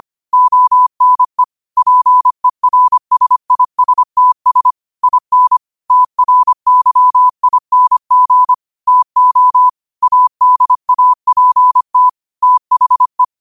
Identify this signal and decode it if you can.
Morse code, heard as a keyed tone, about 25 wpm, 1000 Hz. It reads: ONE PERSISTS IN TRYING TO ADAPT THE
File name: qotdfiles / 25wpm